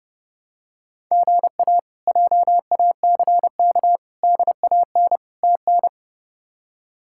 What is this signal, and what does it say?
Morse code, heard as a keyed tone, about 30 wpm, 700 Hz. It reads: GA JACK DAN TN
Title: K1USN SST QSO Practice - 30wpm